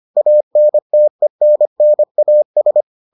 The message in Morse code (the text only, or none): ANTENNAS